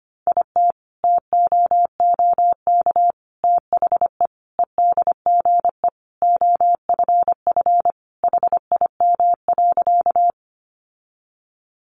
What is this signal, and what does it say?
Morse code, heard as a keyed tone, about 25 wpm, 700 Hz. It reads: IT TOOK THE EDGE OFF HIM.
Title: Sentences from Top 1300 Words 25wpm